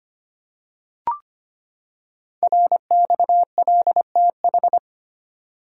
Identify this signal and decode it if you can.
Morse code, heard as a keyed tone, about 25 wpm, 700 Hz. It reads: RXLTH
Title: Code Groups - 5 Letters - 25wpm